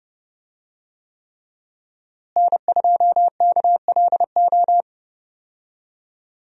Morse code, heard as a keyed tone, about 32 wpm, 700 Hz. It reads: N2KLO
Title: USA Call Signs - 5 Characters - Repeated 3x - 30wpm